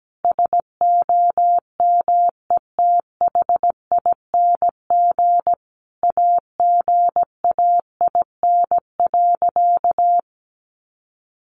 Morse code, heard as a keyed tone, about 17 wpm, 700 Hz. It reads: SOMETHING AGAIN.
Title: Sentences from Top 1100 Words 17wpm